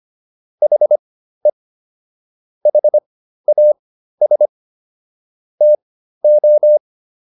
Morse code, heard as a keyed tone, about 25 wpm, spaced as for 13 wpm, 600 Hz. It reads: HE HAS TO